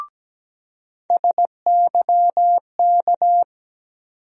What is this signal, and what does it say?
Morse code, heard as a keyed tone, about 17 wpm, 700 Hz. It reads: SYK
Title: USA Call Sign Postfixes - 3 Letters 17wpm